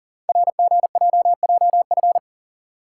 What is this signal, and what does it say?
Morse code, heard as a keyed tone, about 44 wpm, 700 Hz. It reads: RGJJF
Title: Code Groups - 5 Letters - 40wpm